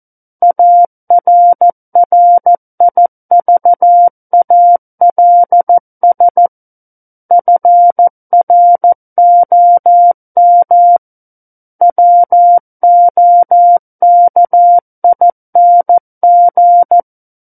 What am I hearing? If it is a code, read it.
Morse code, heard as a keyed tone, about 14 wpm, 700 Hz. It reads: ARRIVALS FROM WOKING
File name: Book One / 05-Chapter 5 - 14 WPM